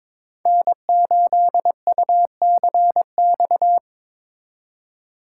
Morse code, heard as a keyed tone, about 22 wpm, 700 Hz. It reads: N8UCX
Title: USA Call Signs - 4x Character Spacing - 22wpm